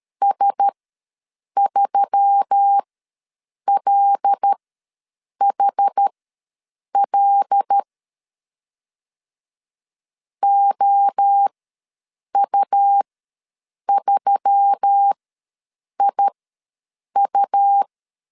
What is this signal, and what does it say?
Morse code, heard as a keyed tone, about 13 wpm, spaced as for 7 wpm, 800 Hz. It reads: S3LHL OU3IU
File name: LEZIONI / 5 LBVC / 5 L ws 12wpm 2958